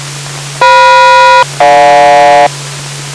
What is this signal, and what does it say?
Aircraft SELCAL, heard as two pairs of tones. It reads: FM-GJ